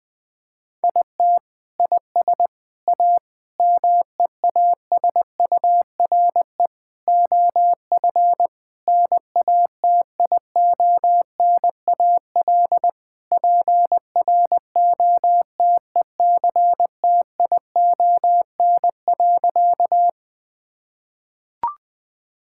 Morse code, heard as a keyed tone, about 21 wpm, 700 Hz. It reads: IT IS A MEASURE OF NATIONAL PROTECTION.
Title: Sentences from Top 1600 Words 20wpm